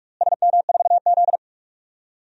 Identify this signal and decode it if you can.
Morse code, heard as a keyed tone, about 46 wpm, 700 Hz. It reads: SM4Z
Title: DXCC-100 Call Signs - 4 Characters - Repeated 2x - 45wpm